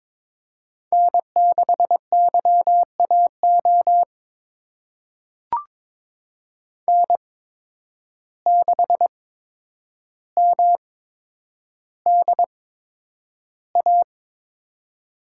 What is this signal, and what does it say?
Morse code, heard as a keyed tone, about 22 wpm, spaced as for 9 wpm, 700 Hz. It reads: N6YAO N6MDA
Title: USA Call Signs - 5 Characters - 8x Character Spacing - 22wpm